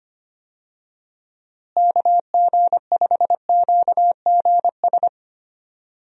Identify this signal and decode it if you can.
Morse code, heard as a keyed tone, about 25 wpm, 700 Hz. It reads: KG5QGS
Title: USA Call Signs - 6 Characters - Repeated 3x - 25wpm